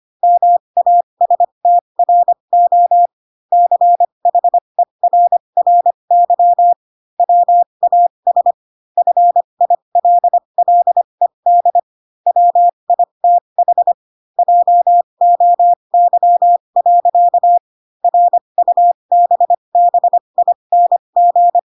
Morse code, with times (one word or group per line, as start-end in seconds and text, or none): 0.2-3.1 MASTRO
3.5-6.7 CHERRY
7.2-8.5 WAS
9.0-11.8 FILLED
12.3-13.9 WITH
14.4-17.6 JOY.
18.0-21.6 RUBBING